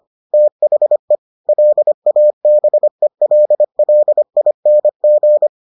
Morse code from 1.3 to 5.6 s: LABELLING

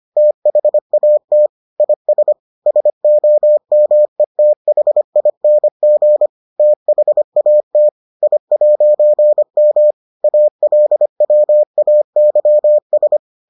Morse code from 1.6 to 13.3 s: IS SOMETHING THAT I'M ALWAYS

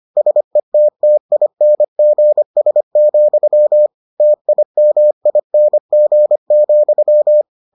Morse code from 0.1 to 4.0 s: SETTINGS,

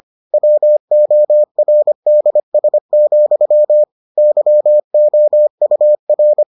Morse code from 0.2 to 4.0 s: WORDS,

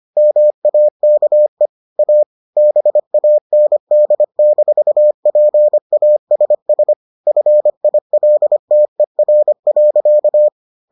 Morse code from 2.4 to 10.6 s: BAND-PASS FILTER.

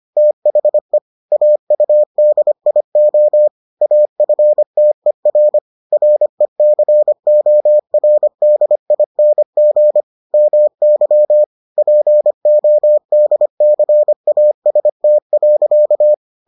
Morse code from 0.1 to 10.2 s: THE AUDIO AFTER RECORDING

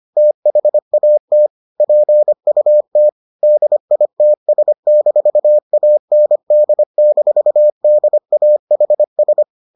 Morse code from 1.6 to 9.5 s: PUT DITS-AND-DAHS

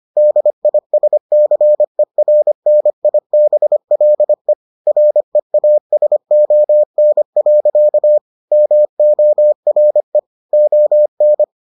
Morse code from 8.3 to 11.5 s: MORE ON

